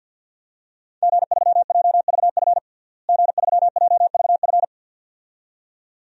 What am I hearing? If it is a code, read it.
Morse code, heard as a keyed tone, about 51 wpm, 700 Hz. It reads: G3JVF G3JVF
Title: DXCC-100 Call Signs - 5 Characters - Repeated 2x - 50wpm